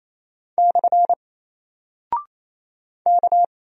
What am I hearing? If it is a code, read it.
Morse code, heard as a keyed tone, about 27 wpm, 700 Hz. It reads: / K
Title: Single Character (Letter:Number:Punctuation:Prosign) 28wpm